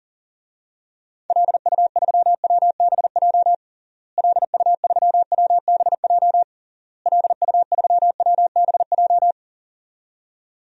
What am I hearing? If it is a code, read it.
Morse code, heard as a keyed tone, about 37 wpm, 700 Hz. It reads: LU3WBJ LU3WBJ LU3WBJ
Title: DXCC-100 Call Signs - 6 Characters - Repeated 3x - 40wpm